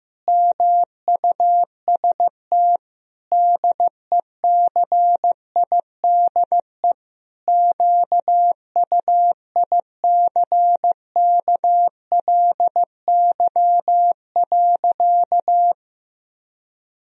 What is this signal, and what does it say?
Morse code, heard as a keyed tone, about 15 wpm, 700 Hz. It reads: MUST DECIDE QUICKLY.
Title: Sentences from Top 1000 Words - Review Part 3 - 15wpm